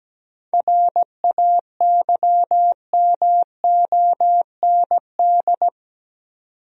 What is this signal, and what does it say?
Morse code, heard as a keyed tone, about 17 wpm, 700 Hz. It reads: RAYMOND
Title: USA Common Names - 17wpm